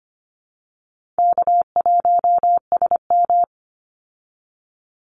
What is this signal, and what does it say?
Morse code, heard as a keyed tone, about 25 wpm, 700 Hz. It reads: K1SM
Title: Call Signs Repeated 3x - 25wpm